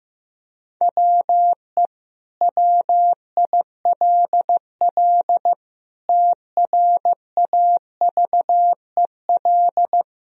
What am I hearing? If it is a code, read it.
Morse code, heard as a keyed tone, about 15 wpm, 700 Hz. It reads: WE WILL TRAVEL
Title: Sentences from Top 1000 Words - Review Part 2 - 15wpm